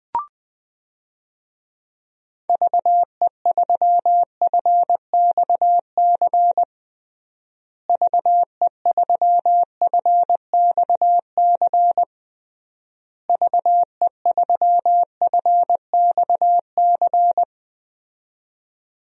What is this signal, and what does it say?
Morse code, heard as a keyed tone, about 20 wpm, 700 Hz. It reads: VE3FXC VE3FXC VE3FXC
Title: DXCC-100 Call Signs - 6 Characters - Repeated 3x - 20wpm